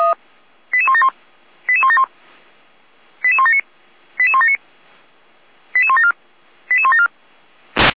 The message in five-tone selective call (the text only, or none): 88171 88171 88179 88179 88174 88174